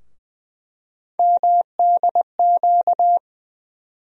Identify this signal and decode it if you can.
Morse code, heard as a keyed tone, about 20 wpm, 700 Hz. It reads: MDQ
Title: USA Call Sign Postfixes - 3 Letters 20wpm